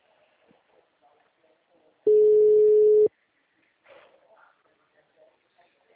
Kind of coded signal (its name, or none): telephone call progress tone